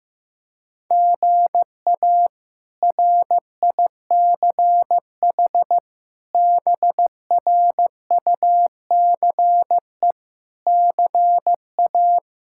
Morse code, heard as a keyed tone, about 15 wpm, 700 Hz. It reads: GA RICH BRUCE CA
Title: K1USN SST QSO Practice - 15wpm